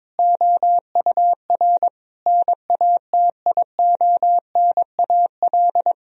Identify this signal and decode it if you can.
Morse code, heard as a keyed tone, about 22 wpm, 700 Hz. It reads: OUR NATIONAL
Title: Sentences from Top 1000 Words - Review Part 4 - 22wpm